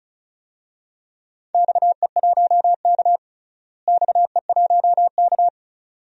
Morse code, heard as a keyed tone, about 36 wpm, 700 Hz. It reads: XE1K XE1K
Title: DXCC-100 Call Signs - 4 Characters - Repeated 2x - 35wpm